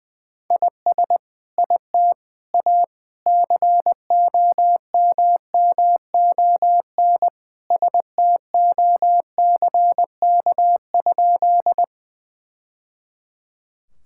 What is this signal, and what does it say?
Morse code, heard as a keyed tone, about 20 wpm, 700 Hz. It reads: IS IT A COMMON STOCK?
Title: Sentences from Top 1000 Words - Review Part 5 - 20wpm